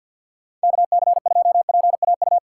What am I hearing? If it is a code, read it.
Morse code, heard as a keyed tone, about 51 wpm, 700 Hz. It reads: KC2PAU
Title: USA Call Signs - 6 Characters - 3x Character Spacing - 50wpm